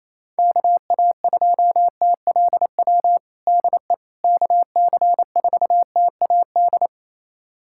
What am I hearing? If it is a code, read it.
Morse code, heard as a keyed tone, about 28 wpm, 700 Hz. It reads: KA2TLW DE KC4TAD
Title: USA Callsign DE Callsign - 6 Characters - 28wpm